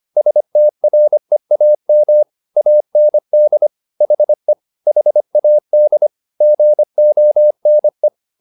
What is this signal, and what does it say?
Morse code, heard as a keyed tone, about 25 wpm, 600 Hz. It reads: STREAM AND HE HAD GONE